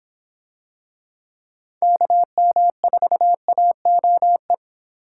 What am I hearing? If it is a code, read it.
Morse code, heard as a keyed tone, about 26 wpm, 700 Hz. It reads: KM4AOE